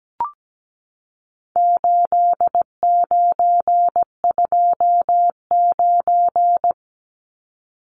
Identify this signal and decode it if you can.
Morse code, heard as a keyed tone, about 17 wpm, 700 Hz. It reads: 8929
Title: Numbers - 4-Digit 17wpm